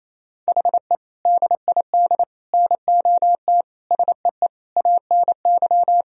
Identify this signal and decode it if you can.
Morse code, heard as a keyed tone, about 28 wpm, 700 Hz. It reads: HE DID NOT SEE ANY
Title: Sentences from Top 1000 Words - Review Part 3 - 28wpm